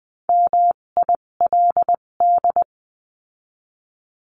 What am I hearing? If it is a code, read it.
Morse code, heard as a keyed tone, about 20 wpm, spaced as for 17 wpm, 700 Hz. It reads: MILD
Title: 257 Lesson 42b - Words 14-20wpm